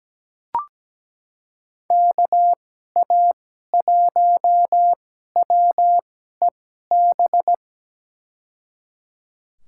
Morse code, heard as a keyed tone, about 17 wpm, spaced as for 12 wpm, 700 Hz. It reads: KA1WEB
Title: USA Call Signs - 6 Characters - 2x Character Spacing - 17wpm